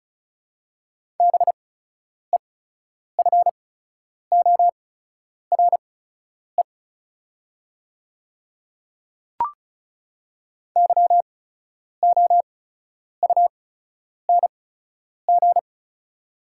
Morse code, heard as a keyed tone, about 36 wpm, spaced as for 10 wpm, 690 Hz. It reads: BEFORE YOUNG
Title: 8x Character Spacing - Top 500 Words - 35wpm